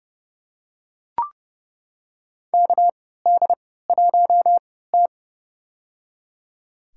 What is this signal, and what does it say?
Morse code, heard as a keyed tone, about 30 wpm, spaced as for 17 wpm, 700 Hz. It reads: KD1T
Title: USA Call Signs - 4 Characters - 3x Character Spacing - 30wpm